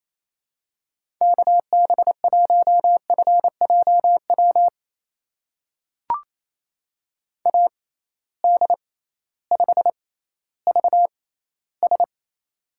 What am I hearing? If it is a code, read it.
Morse code, heard as a keyed tone, about 28 wpm, spaced as for 15 wpm, 700 Hz. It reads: KB1FJW AD5VS
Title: USA Call Signs - 6x Character Spacing - 28wpm